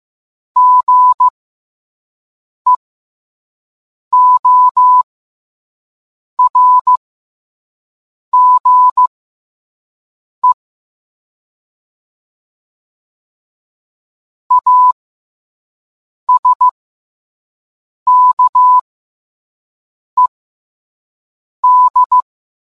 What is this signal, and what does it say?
Morse code, heard as a keyed tone, about 15 wpm, spaced as for 5 wpm, 1000 Hz. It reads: GEORGE ASKED